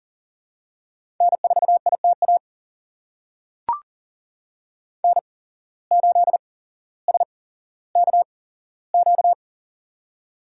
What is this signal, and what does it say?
Morse code, heard as a keyed tone, about 39 wpm, spaced as for 17 wpm, 700 Hz. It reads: N4ITA N8SKQ
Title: USA Call Signs - 5 Characters - 8x Character Spacing - 40wpm